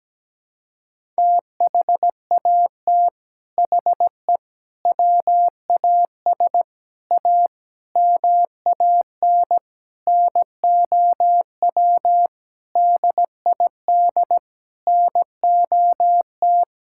Morse code, heard as a keyed tone, about 17 wpm, 700 Hz. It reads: THAT HE WAS A MAN NOW DID NOT
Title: Sentences from Top 1000 Words - Review Part 6 - 17wpm